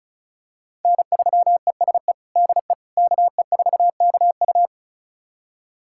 Morse code, heard as a keyed tone, about 34 wpm, 700 Hz. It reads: N3ESE DE KE4KU
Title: USA Callsign DE Callsign - 5 Characters - 35wpm